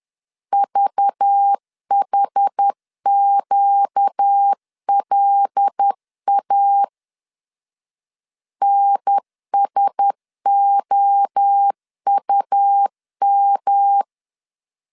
Morse code, heard as a keyed tone, about 10 wpm, 800 Hz. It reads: VHQLA NSOUM
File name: LEZIONI / 6 PKX0 / 6 PKX0 10wpm 2707